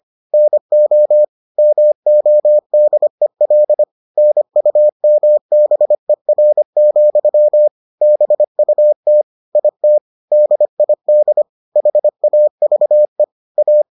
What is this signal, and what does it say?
Morse code, heard as a keyed tone, about 25 wpm, 600 Hz. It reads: NO MODEL NUMBER, BUT IT DID HAVE A